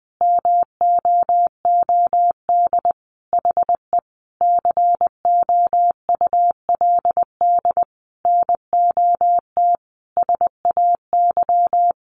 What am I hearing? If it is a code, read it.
Morse code, heard as a keyed tone, about 19 wpm, 700 Hz. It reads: MOOD HE COULD NOT SAY
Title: Sentences from Top 1100 Words 20wpm